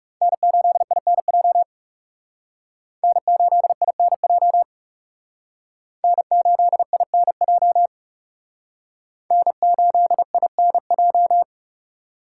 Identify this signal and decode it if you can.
Morse code, heard as a keyed tone, about 37 wpm, 700 Hz. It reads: N8INJ N8INJ N8INJ N8INJ
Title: Speed Racing - USA Call Signs 30wpm